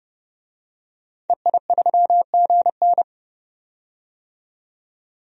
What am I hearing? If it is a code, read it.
Morse code, heard as a keyed tone, about 30 wpm, 700 Hz. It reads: EI3GN
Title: DXCC-100 Call Signs - 5 Characters - 30wpm